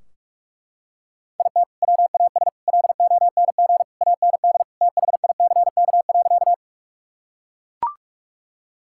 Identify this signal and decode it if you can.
Morse code, heard as a keyed tone, about 41 wpm, 700 Hz. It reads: IT WAS LONG AND THICK.
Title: Sentences from Top 1600 Words 45wpm